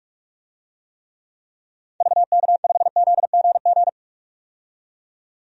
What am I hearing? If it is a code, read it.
Morse code, heard as a keyed tone, about 46 wpm, 700 Hz. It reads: VK5ZGG
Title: DXCC-100 Call Signs - 6 Characters - 45wpm